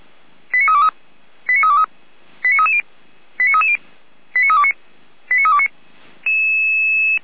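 Five-tone selective call (ZVEI), heard as quick runs of five tones, tones of about 70 ms. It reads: 89323 89323 89330 89330 89329 89329
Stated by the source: ZVEI five-tone call 89501 received on an SDR.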